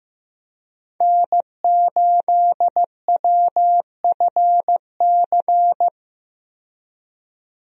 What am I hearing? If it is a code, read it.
Morse code, heard as a keyed tone, about 15 wpm, 700 Hz. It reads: N8WFC
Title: USA Call Signs - 5 Characters - 15wpm